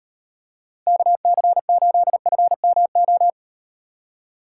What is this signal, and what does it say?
Morse code, heard as a keyed tone, about 38 wpm, 700 Hz. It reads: KC8FMO